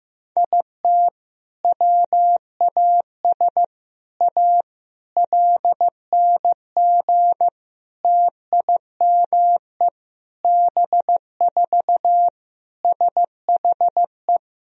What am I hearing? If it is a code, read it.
Morse code, heard as a keyed tone, about 16 wpm, 700 Hz. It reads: IT WAS A LNG TIME B4 SHE
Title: Sentences from Top 500 Words and Abbreviations 15wpm